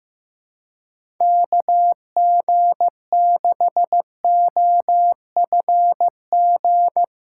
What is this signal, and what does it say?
Morse code, heard as a keyed tone, about 16 wpm, 700 Hz. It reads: KG6OFG
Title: USA Callsign DE Callsign 15wpm